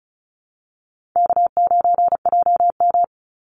Morse code, heard as a keyed tone, about 34 wpm, 700 Hz. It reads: K9JM
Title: Easy Call Signs Repeated 3x - 35wpm